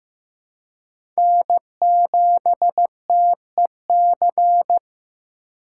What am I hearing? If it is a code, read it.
Morse code, heard as a keyed tone, about 15 wpm, 700 Hz. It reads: N7TEC
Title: USA Call Signs - 5 Characters - 4x Character Spacing - 15wpm